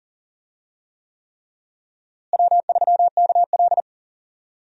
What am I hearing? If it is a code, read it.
Morse code, heard as a keyed tone, about 40 wpm, 700 Hz. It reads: W3KL
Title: USA Call Signs - 4 Characters - Repeated 3x - 40wpm